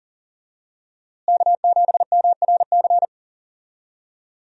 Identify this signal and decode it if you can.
Morse code, heard as a keyed tone, about 40 wpm, 700 Hz. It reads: K7MRC
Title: USA Call Signs - 5 Characters - Repeated 2x - 40wpm